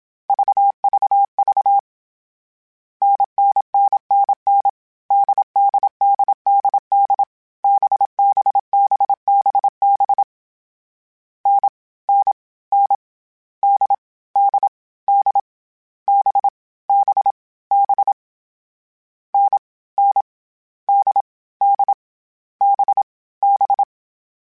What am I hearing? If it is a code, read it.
Morse code, heard as a keyed tone, about 27 wpm, 800 Hz. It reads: VVV NNNNN DDDDD BBBBB N N N D D D B B B N N D D B B
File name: Short Runs / Short Runs-mp3s / NDB NDB6 @25